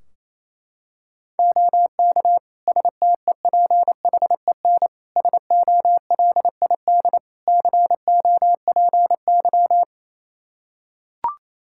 Morse code, heard as a keyed tone, about 27 wpm, 700 Hz. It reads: OK STEPHEN SOLID COPY
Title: K1USN SST QSO Practice 28wpm